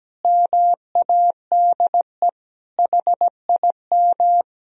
Morse code, heard as a keyed tone, about 17 wpm, 700 Hz. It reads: MADE HIM
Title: Sentences from Top 1000 Words - Review Part 4 - 17wpm